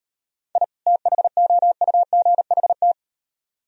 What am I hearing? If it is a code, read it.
Morse code, heard as a keyed tone, about 38 wpm, 700 Hz. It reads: I THOUGHT